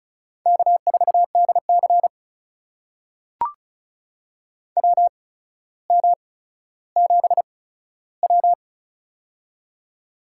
Morse code, heard as a keyed tone, about 36 wpm, spaced as for 15 wpm, 700 Hz. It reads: K4DC WM7W
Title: USA Call Signs - 4 Characters - 8x Character Spacing - 35wpm